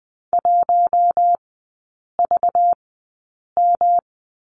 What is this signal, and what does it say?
Morse code, heard as a keyed tone, about 20 wpm, spaced as for 8 wpm, 700 Hz. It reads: 1VM
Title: 191 Lesson 31e - Letter v - Three Characters 20wpm